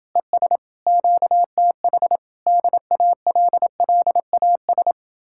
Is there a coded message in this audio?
Yes: Morse code from 0.2 to 4.9 s, heard as a keyed tone, about 31 wpm, 700 Hz. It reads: ES QTH DALLAS